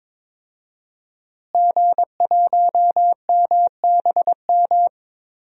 Morse code, heard as a keyed tone, about 21 wpm, 700 Hz. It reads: G1MBM
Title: DXCC-100 Call Signs - 5 Characters - Repeated 2x - 22wpm